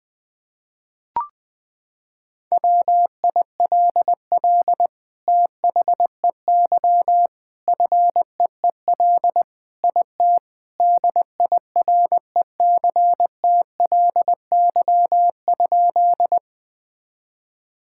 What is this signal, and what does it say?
Morse code, heard as a keyed tone, about 20 wpm, 700 Hz. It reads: WILL THEY FEEL IT DIRECTLY?
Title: Sentences from Top 1000 Words - Review Part 5 - 20wpm